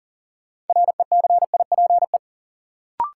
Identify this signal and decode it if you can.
Morse code, heard as a keyed tone, about 40 wpm, 700 Hz. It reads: RECIPE
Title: Sentences from Top 1200 Words - New Word Drill - 40wpm